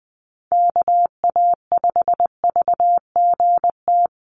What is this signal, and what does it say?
Morse code, heard as a keyed tone, about 20 wpm, 700 Hz. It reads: KA5VGT
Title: USA Call Signs Repeated 3x 20wpm